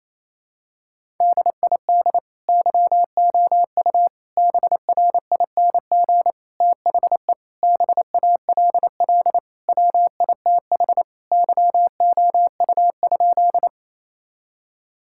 Morse code, heard as a keyed tone, about 28 wpm, 700 Hz. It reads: DID YOU BRING THE BALL WITH YOU?
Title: Sentences from Top 1000 Words - Review Part 6 - 28wpm